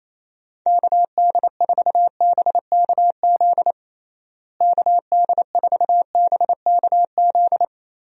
Morse code, heard as a keyed tone, about 28 wpm, 700 Hz. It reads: KD4BKZ KD4BKZ
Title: USA Call Signs - 6 Characters - Repeated 3x - 28wpm